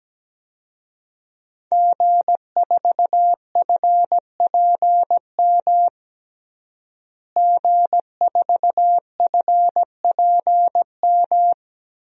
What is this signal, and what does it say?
Morse code, heard as a keyed tone, about 17 wpm, 700 Hz. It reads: G4FPM G4FPM
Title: DXCC-100 Call Signs - 5 Characters - Repeated 2x - 17wpm